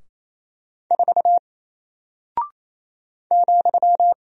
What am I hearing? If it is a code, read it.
Morse code, heard as a keyed tone, about 28 wpm, 700 Hz. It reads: 4 ,
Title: Single Character (Letter:Number:Punctuation:Prosign) 28wpm